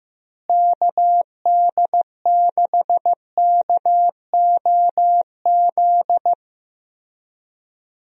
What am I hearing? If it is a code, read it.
Morse code, heard as a keyed tone, about 15 wpm, 700 Hz. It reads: KD6KOZ
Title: USA Call Signs - 6 Characters - Repeated 2x - 15wpm